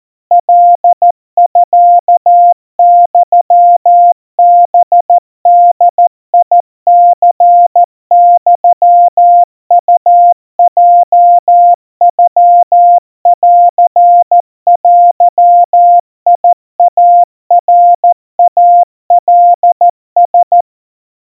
Morse code (Japanese Wabun code), heard as a keyed tone, about 14 wpm, 700 Hz. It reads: カミユハボニユウヲノンデイナイカラ
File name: kawa / Morse sample / kamyu 50 cpm 700 Hz Crispy